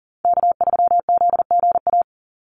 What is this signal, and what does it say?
Morse code, heard as a keyed tone, about 40 wpm, 700 Hz. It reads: K3ZGA
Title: Call Signs Repeated 3x - 40wpm